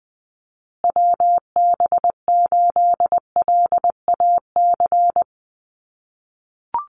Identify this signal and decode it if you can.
Morse code, heard as a keyed tone, about 20 wpm, 700 Hz. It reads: WB8LAC
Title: USA Call Signs Repeated 2x 20wpm